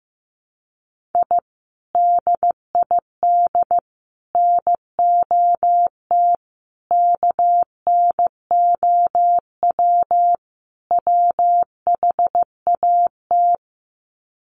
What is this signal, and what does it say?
Morse code, heard as a keyed tone, about 15 wpm, 700 Hz. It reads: I DID NOT KNOW WHAT
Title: Sets of 5 Words 15wpm